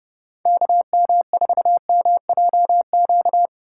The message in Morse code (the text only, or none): KM4MJQ